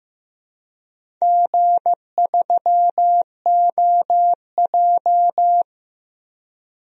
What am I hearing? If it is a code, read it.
Morse code, heard as a keyed tone, about 15 wpm, 700 Hz. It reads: G3OJ
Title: DXCC-100 Call Signs - 4 Characters - Repeated 2x - 15wpm